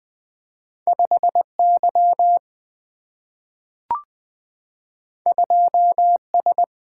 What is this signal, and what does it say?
Morse code, heard as a keyed tone, about 20 wpm, 700 Hz. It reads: 5Y 2S
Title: Single-Letter Single-Number 20wpm